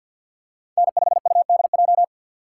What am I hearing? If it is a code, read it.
Morse code, heard as a keyed tone, about 48 wpm, 700 Hz. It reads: N5UDJ
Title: USA Call Signs - 5 Characters - 3x Character Spacing - 50wpm